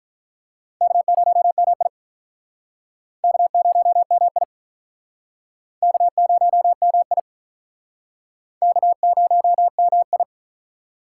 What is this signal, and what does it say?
Morse code, heard as a keyed tone, about 48 wpm, 700 Hz. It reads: K0MI K0MI K0MI K0MI